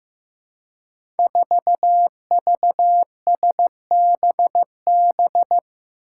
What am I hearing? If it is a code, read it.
Morse code, heard as a keyed tone, about 15 wpm, 700 Hz. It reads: 4VSBB
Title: Five Challenging Letter-Number Combinations 15wpm